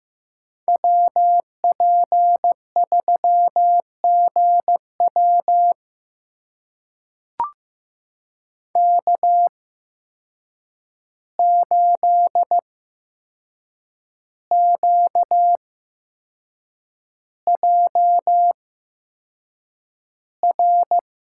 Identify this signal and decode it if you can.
Morse code, heard as a keyed tone, about 15 wpm, spaced as for 6 wpm, 700 Hz. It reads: WP3GW K8QJR
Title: USA Call Signs - 5 Characters - 8x Character Spacing - 15wpm